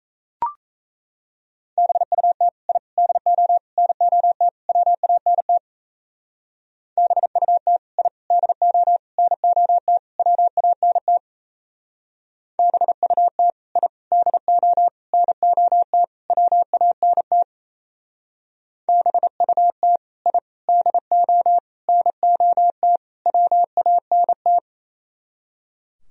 Morse code, heard as a keyed tone, about 35 wpm, 700 Hz. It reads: BUT I DO NOT WANT BUT I DO NOT WANT BUT I DO NOT WANT BUT I DO NOT WANT
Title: Speed-Racing - Sets of 5 Words 28wpm